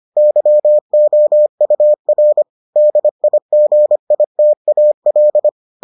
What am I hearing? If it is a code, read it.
Morse code, heard as a keyed tone, about 25 wpm, 600 Hz. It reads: YOUR DIGITAL